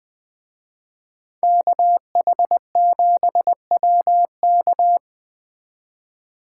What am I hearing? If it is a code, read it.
Morse code, heard as a keyed tone, about 20 wpm, 700 Hz. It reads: KH7WK
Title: Speed Racing - USA Call Signs 17wpm